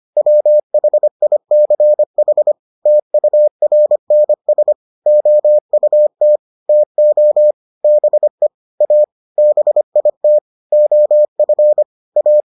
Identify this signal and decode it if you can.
Morse code, heard as a keyed tone, about 25 wpm, 600 Hz. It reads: WHICH TURNS OUT TO BE A BIT OF A